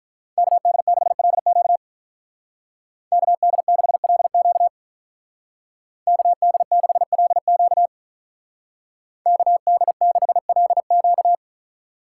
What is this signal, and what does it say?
Morse code, heard as a keyed tone, about 44 wpm, 700 Hz. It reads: KD6LQ KD6LQ KD6LQ KD6LQ